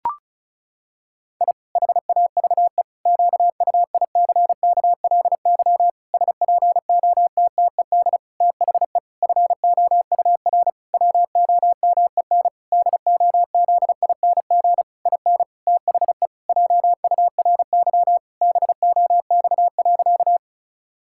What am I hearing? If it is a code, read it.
Morse code, heard as a keyed tone, about 34 wpm, 700 Hz. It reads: I HAVE QUICKLY SPOTTED THE FOUR WOMEN DOZING IN THE JURY BOX.
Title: Pangram Challenge v2 - 35wpm